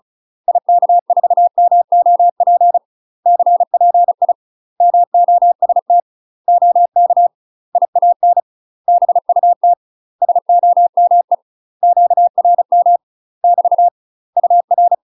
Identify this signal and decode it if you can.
Morse code, heard as a keyed tone, about 34 wpm, 700 Hz. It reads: IK4MOP CPI MOST OK IAN BUT SOME QRM <BT> UR
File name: UK MASTER QSOs TRAINING/35WPM